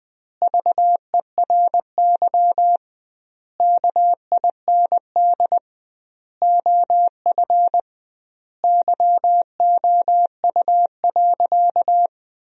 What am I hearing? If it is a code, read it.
Morse code, heard as a keyed tone, about 21 wpm, 700 Hz. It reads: VERY KIND OF YOU.
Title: 2x Word-Spacing - Sentences from Top 200 Words - 20wpm